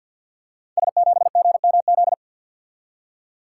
Morse code, heard as a keyed tone, about 52 wpm, 700 Hz. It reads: S7GMZ
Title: Code Groups - 5 Letter:Numbers - 50wpm